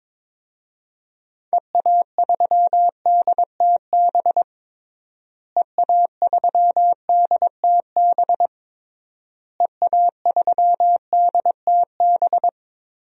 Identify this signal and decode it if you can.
Morse code, heard as a keyed tone, about 22 wpm, 700 Hz. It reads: EA3DTB EA3DTB EA3DTB
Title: DXCC-100 Call Signs - 6 Characters - Repeated 3x - 22wpm